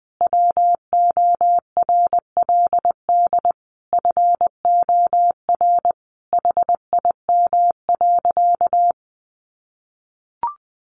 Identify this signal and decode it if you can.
Morse code, heard as a keyed tone, about 20 wpm, 700 Hz. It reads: WORLD FOR HIM.
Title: Sentences from Top 1300 Words 20wpm